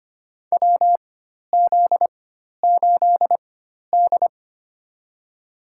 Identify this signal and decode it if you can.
Morse code, heard as a keyed tone, about 25 wpm, spaced as for 12 wpm, 700 Hz. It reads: WZ8D
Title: USA Call Signs - 4x Character Spacing - 25wpm